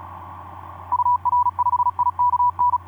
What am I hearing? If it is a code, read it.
Morse code, heard as a keyed tone, about 37 wpm, 980 Hz. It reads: UR5IKN
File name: d44ac ek2rgt etc 10m cw 23nov2010 1202